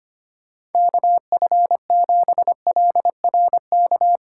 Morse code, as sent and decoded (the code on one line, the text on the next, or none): -.- ..-. --... .-.. .-. -.-
KF7LRK